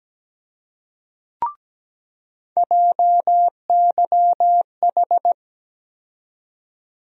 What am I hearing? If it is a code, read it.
Morse code, heard as a keyed tone, about 17 wpm, 700 Hz. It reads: JYH
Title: USA Call Sign Postfixes - 3 Letters 17wpm